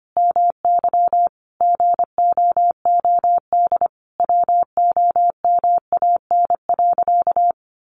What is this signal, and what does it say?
Morse code, heard as a keyed tone, about 25 wpm, 700 Hz. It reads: MY GOOD WOMAN.
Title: Sentences from Top 1100 Words 25wpm